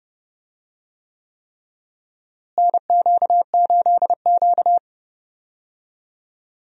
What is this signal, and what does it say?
Morse code, heard as a keyed tone, about 31 wpm, 700 Hz. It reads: NQ8Q
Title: USA Call Signs - 4 Characters - Repeated 3x - 30wpm